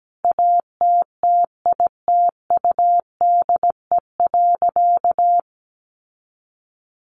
Morse code, heard as a keyed tone, about 17 wpm, 700 Hz. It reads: ATTITUDE.
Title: Sentences from Top 1300 Words 17wpm